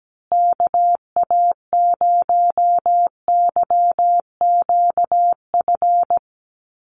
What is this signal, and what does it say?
Morse code, heard as a keyed tone, about 17 wpm, 700 Hz. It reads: KA0YQF
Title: USA Call Signs Repeated 2x 17wpm